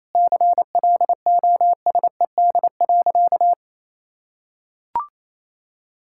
Morse code, heard as a keyed tone, about 28 wpm, 700 Hz. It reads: CLOSED.
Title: Sentences from Top 1600 Words 28wpm